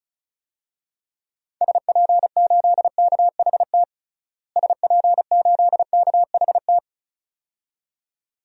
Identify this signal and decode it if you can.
Morse code, heard as a keyed tone, about 36 wpm, 700 Hz. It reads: SP8KHT SP8KHT
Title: DXCC-100 Call Signs - 6 Characters - Repeated 2x - 35wpm